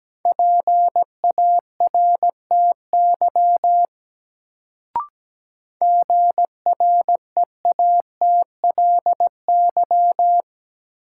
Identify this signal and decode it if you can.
Morse code, heard as a keyed tone, about 17 wpm, 700 Hz. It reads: PARTY GREATLY
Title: Sentences from Top 1400 Words - New Word Drill - 17wpm